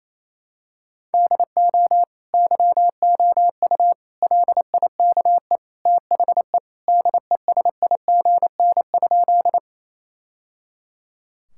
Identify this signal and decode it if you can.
Morse code, heard as a keyed tone, about 28 wpm, 700 Hz. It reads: DO YOU LIKE THE DESIGN?
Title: Sentences from Top 1000 Words - Review Part 5 - 28wpm